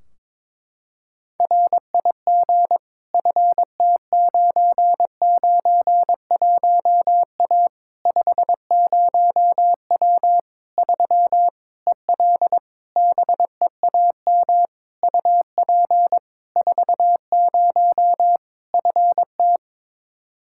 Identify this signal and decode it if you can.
Morse code, heard as a keyed tone, about 22 wpm, 700 Hz. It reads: RIG FT991A 50W 3 EL BEAM UP 40 FT
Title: Practice QSO - Exchange 2 - Info - 22wpm